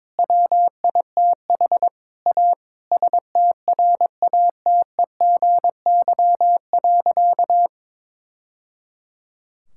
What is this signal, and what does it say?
Morse code, heard as a keyed tone, about 22 wpm, 700 Hz. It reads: WITH A STRATEGY.
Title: Sentences from Top 1000 Words - Review Part 5 - 22wpm